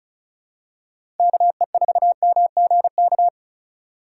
Morse code, heard as a keyed tone, about 34 wpm, 690 Hz. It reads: KE4MGK
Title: USA Call Signs - 6 Characters - 3x Character Spacing - 35wpm